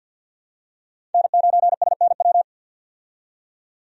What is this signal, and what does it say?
Morse code, heard as a keyed tone, about 50 wpm, 700 Hz. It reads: N9SNW
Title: USA Call Signs - 5 Characters - 50wpm